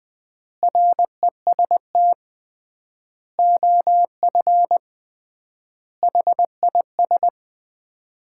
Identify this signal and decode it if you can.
Morse code, heard as a keyed tone, about 20 wpm, 700 Hz. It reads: REST OF HIS
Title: 3x Word-Spacing - Sets of 5 Words - 20wpm